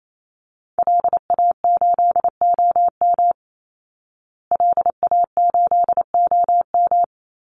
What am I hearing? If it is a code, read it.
Morse code, heard as a keyed tone, about 28 wpm, 700 Hz. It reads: LA8OM LA8OM
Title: Call Signs Repeated 3x - 28wpm